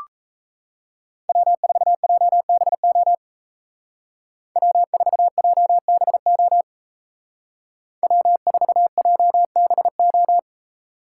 Morse code, heard as a keyed tone, about 33 wpm, 700 Hz. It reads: W4JBO W4JBO W4JBO
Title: Speed Racing - USA 1x3 Call Signs 28wpm